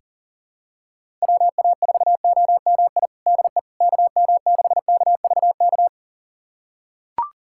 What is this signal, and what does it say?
Morse code, heard as a keyed tone, about 37 wpm, 700 Hz. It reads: WA4OMI DE KM6KVK
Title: USA Callsign DE Callsign - 6 Characters - 40wpm